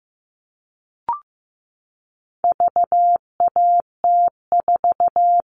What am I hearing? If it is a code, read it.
Morse code, heard as a keyed tone, about 15 wpm, 700 Hz. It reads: VAT4
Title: All Cut Numbers - 4-Digit 15wpm